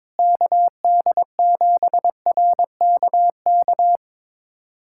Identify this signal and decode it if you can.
Morse code, heard as a keyed tone, about 22 wpm, 700 Hz. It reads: KD7RKK
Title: USA Callsign DE Callsign - 6 Characters - 22wpm